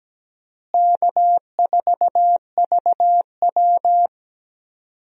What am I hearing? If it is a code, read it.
Morse code, heard as a keyed tone, about 17 wpm, 700 Hz. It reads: K4VW
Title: USA Call Signs - 4 Characters - Repeated 2x - 17wpm